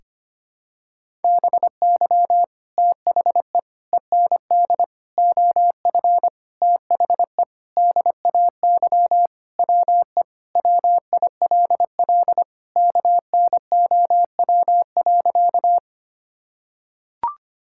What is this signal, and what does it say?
Morse code, heard as a keyed tone, about 25 wpm, 700 Hz. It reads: BY THE END OF THE DAY WE WILL KNOW.
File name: practice / 25wpm / Sentences with 4-Word Phrases - 25wpm